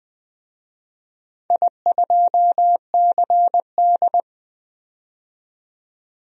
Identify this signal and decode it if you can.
Morse code, heard as a keyed tone, about 20 wpm, 700 Hz. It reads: I2CD
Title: DXCC-100 Call Signs - 4 Characters - 20wpm